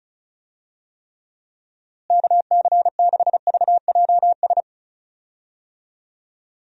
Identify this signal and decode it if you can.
Morse code, heard as a keyed tone, about 36 wpm, 700 Hz. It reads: KC6VJS
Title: USA Call Signs - 6 Characters - Repeated 2x - 35wpm